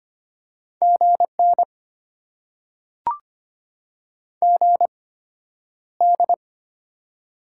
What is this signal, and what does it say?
Morse code, heard as a keyed tone, about 25 wpm, spaced as for 11 wpm, 700 Hz. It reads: GN GD
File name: Top 90 QSO Elements - 8x Character Spacing - 25wpm